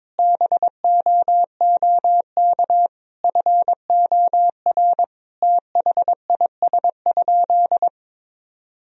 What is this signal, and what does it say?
Morse code, heard as a keyed tone, about 22 wpm, 700 Hz. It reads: BOOK FOR THIS?
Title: Sentences from Top 1000 Words - Review Part 4 - 22wpm